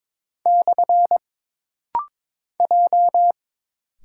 Morse code, heard as a keyed tone, about 22 wpm, 700 Hz. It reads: / J
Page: This rapid-fire timing is geared to push your ICR (Instant Character Recognition) skills.